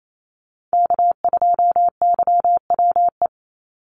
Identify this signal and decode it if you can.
Morse code, heard as a keyed tone, about 28 wpm, 700 Hz. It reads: K2YWE
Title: Call Signs - 28wpm